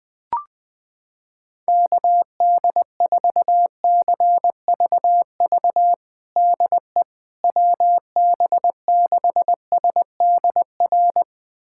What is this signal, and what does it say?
Morse code, heard as a keyed tone, about 21 wpm, 700 Hz. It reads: KD4CVV DE WB6SDR
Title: USA Callsign DE Callsign - 6 Characters - 20wpm